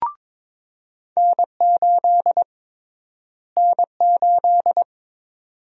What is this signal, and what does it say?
Morse code, heard as a keyed tone, about 22 wpm, 700 Hz. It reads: N8 N8
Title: USA Call Sign Prefixes - Repeated 2x - 22wpm